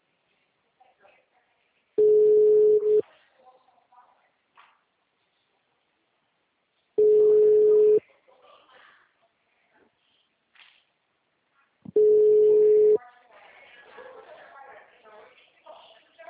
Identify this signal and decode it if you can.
Ringback tone (Europe, 425 Hz), with speech.
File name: out- -2022- - - .3453